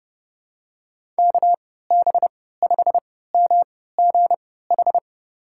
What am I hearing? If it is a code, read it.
Morse code, heard as a keyed tone, about 30 wpm, spaced as for 17 wpm, 700 Hz. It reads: KB5MGH
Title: USA Call Signs - 6 Characters - 3x Character Spacing - 30wpm